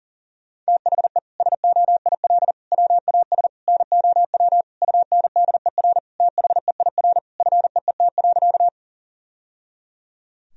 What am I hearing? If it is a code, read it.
Morse code, heard as a keyed tone, about 40 wpm, 700 Hz. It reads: THE SOIL WAS NOW UNDER THEIR FEET.
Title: Sentences from Top 1000 Words - Review Part 4 - 40wpm